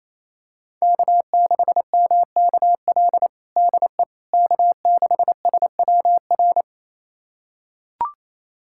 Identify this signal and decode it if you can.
Morse code, heard as a keyed tone, about 28 wpm, 700 Hz. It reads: K6MKL DE K6SWR
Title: USA Callsign DE Callsign - 5 Characters - 28wpm